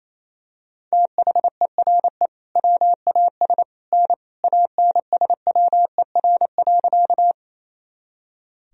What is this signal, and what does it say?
Morse code, heard as a keyed tone, about 28 wpm, 700 Hz. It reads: THERE WAS N ANSWER.
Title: Sentences from Top 300 Words and Abbreviations 28wpm